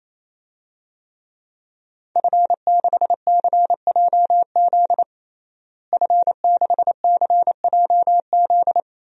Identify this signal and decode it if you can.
Morse code, heard as a keyed tone, about 27 wpm, 700 Hz. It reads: F6CJZ F6CJZ
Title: DXCC-100 Call Signs - 5 Characters - Repeated 3x - 28wpm